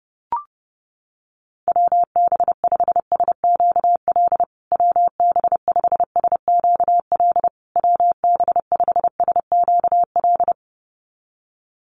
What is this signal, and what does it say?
Morse code, heard as a keyed tone, about 30 wpm, 700 Hz. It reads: WB5SQL WB5SQL WB5SQL
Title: USA Call Signs Repeated 3x 30wpm